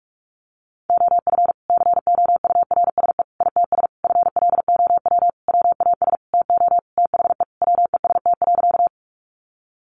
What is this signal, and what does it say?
Morse code, heard as a keyed tone, about 44 wpm, 700 Hz. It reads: OF COURSE ITS FLOW WAS TO THE WEST.
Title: Sentences from Top 1200 Words 45wpm